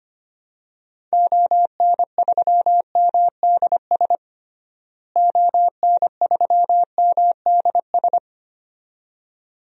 Morse code, heard as a keyed tone, about 25 wpm, 700 Hz. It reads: ON3MDS ON3MDS
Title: DXCC-100 Call Signs - 6 Characters - Repeated 2x - 25wpm